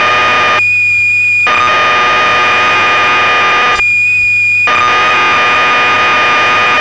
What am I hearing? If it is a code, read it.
Cassette data (Kansas City type).
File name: magazines / Acorn User / 1985-04 Electron Programs